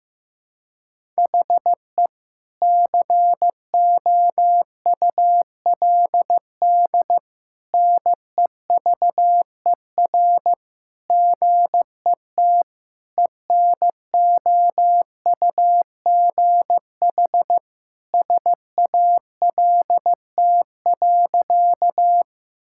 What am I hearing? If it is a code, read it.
Morse code, heard as a keyed tone, about 15 wpm, 700 Hz. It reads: HE COULD NEVER GET ENOUGH SALT.
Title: Sentences from Top 1000 Words - Review Part 5 - 15wpm